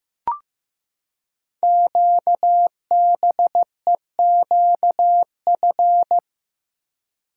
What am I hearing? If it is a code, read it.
Morse code, heard as a keyed tone, about 15 wpm, 690 Hz. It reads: QBEQF
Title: Code Groups - 5 Letters - 15wpm